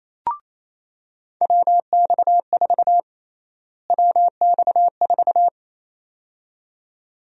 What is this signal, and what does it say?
Morse code, heard as a keyed tone, about 27 wpm, 700 Hz. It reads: WX4 WX4
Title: USA Call Sign Prefixes - Repeated 2x - 28wpm